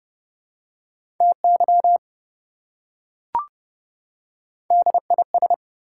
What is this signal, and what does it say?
Morse code, heard as a keyed tone, about 27 wpm, 700 Hz. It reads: TY DIS